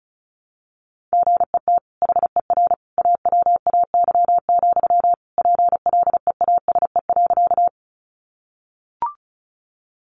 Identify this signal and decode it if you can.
Morse code, heard as a keyed tone, about 34 wpm, 700 Hz. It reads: GET HER AWAY, PLEASE.
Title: Sentences from Top 1500 Words 35wpm